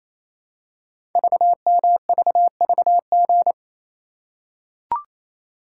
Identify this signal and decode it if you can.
Morse code, heard as a keyed tone, about 28 wpm, 700 Hz. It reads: VMVVG
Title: Code Groups - 5 Letters - 28wpm